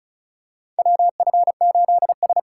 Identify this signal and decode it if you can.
Morse code, heard as a keyed tone, about 36 wpm, 700 Hz. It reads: WF8S
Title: USA Call Signs - 4 Characters - 4x Character Spacing - 35wpm